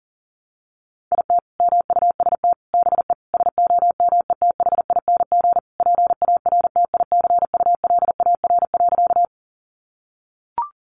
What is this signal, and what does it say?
Morse code, heard as a keyed tone, about 40 wpm, 700 Hz. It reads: IT MUST BE SOMETHING PARTICULAR.
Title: Sentences from Top 600 Words 40wpm